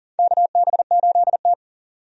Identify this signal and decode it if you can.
Morse code, heard as a keyed tone, about 38 wpm, 700 Hz. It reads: KB8T
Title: USA Call Signs - 4 Characters - 40wpm